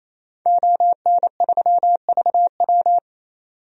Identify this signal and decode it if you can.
Morse code, heard as a keyed tone, about 28 wpm, 700 Hz. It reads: ON3VW
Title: DXCC-100 Call Signs - 5 Characters - 28wpm